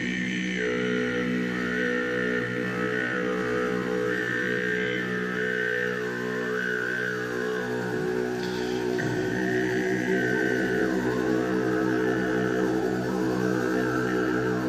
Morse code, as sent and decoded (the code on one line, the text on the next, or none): none